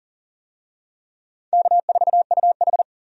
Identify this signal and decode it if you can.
Morse code, heard as a keyed tone, about 40 wpm, 700 Hz. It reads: K4UH
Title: USA Call Signs - 4 Characters - 40wpm